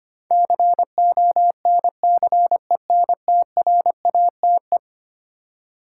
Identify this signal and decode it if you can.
Morse code, heard as a keyed tone, about 25 wpm, 700 Hz. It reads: CONCENTRATE
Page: morse-code-ninja-practice-25wpm